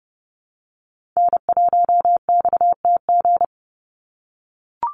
Morse code, heard as a keyed tone, about 30 wpm, 700 Hz. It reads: N1XTG
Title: USA Call Signs 30wpm